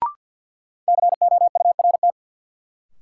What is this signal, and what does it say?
Morse code, heard as a keyed tone, about 52 wpm, 700 Hz. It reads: COURT
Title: Sentences from Top 1600 Words - New Word Drill Alt - 50wpm